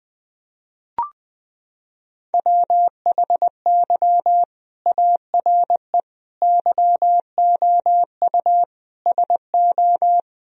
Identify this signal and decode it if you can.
Morse code, heard as a keyed tone, about 21 wpm, 700 Hz. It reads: WHY ARE YOU SO
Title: Sentences from Top 1000 Words - Review Part 2 - 20wpm